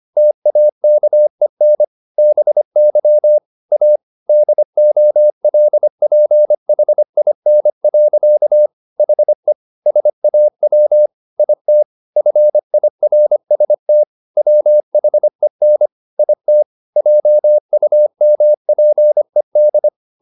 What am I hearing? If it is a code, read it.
Morse code, heard as a keyed tone, about 25 wpm, 600 Hz. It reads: TAKEN BY A DOLPHIN. HE SAW IT FIRST WHEN IT JUMPED